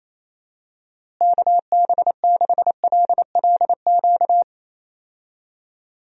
Morse code, heard as a keyed tone, about 28 wpm, 700 Hz. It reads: KB6LLQ
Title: USA Call Signs - 6 Characters - Repeated 3x - 28wpm